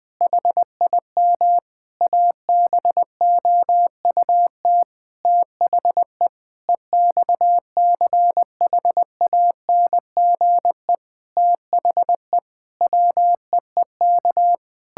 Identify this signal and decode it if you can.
Morse code, heard as a keyed tone, about 21 wpm, 700 Hz. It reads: HIM ABOUT THE EXCHANGE THE WEEK
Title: Sentences from Top 1000 Words - Review Part 6 - 20wpm